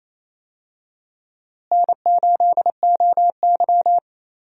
Morse code, heard as a keyed tone, about 28 wpm, 700 Hz. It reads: N8OY
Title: USA Call Signs - 4 Characters - Repeated 2x - 28wpm